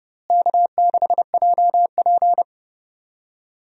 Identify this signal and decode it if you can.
Morse code, heard as a keyed tone, about 30 wpm, 700 Hz. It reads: K6JP
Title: USA Call Signs - 4 Characters - Repeated 3x - 30wpm